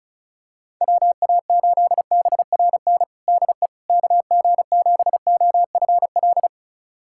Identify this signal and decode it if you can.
Morse code, heard as a keyed tone, about 34 wpm, 700 Hz. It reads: WA8BRN DE KG7OFL
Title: USA Callsign DE Callsign - 6 Characters - 35wpm